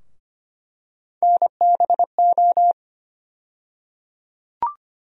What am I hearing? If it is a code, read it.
Morse code, heard as a keyed tone, about 25 wpm, 700 Hz. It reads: NBO
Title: USA Call Sign Postfixes - 3 Letters 25wpm